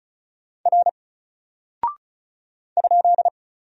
Morse code, heard as a keyed tone, about 34 wpm, 700 Hz. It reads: R ?
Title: Single Character (Letter:Number:Punctuation:Prosign) 35wpm